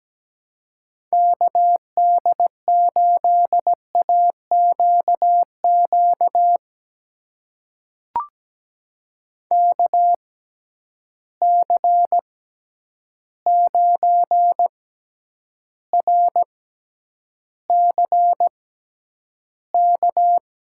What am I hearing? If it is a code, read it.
Morse code, heard as a keyed tone, about 17 wpm, spaced as for 9 wpm, 700 Hz. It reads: KD8AQQ KC9RCK
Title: USA Call Signs - 6 Characters - 6x Character Spacing - 17wpm